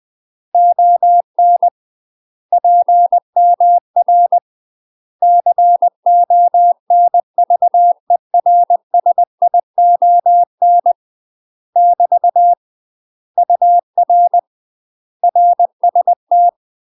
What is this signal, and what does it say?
Morse code, heard as a keyed tone, about 20 wpm, 700 Hz. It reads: ON PMR CONVERSION <BT> UR RST
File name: UK MASTER QSOs TRAINING / 20WPM / UK MASTER QSOs 0069